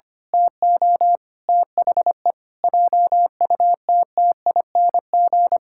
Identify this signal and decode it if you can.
Morse code, heard as a keyed tone, about 25 wpm, 700 Hz. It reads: TO THE JUTTING
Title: Pangram Challenge v2 - 25wpm